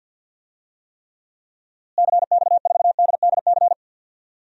Morse code, heard as a keyed tone, about 51 wpm, 700 Hz. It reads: CX4DDC
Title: DXCC-100 Call Signs - 6 Characters - Repeated 3x - 50wpm